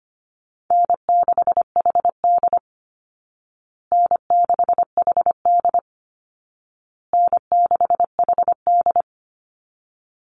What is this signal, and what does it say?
Morse code, heard as a keyed tone, about 25 wpm, 700 Hz. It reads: N6HD N6HD N6HD
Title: Call Signs Repeated 2x - 25wpm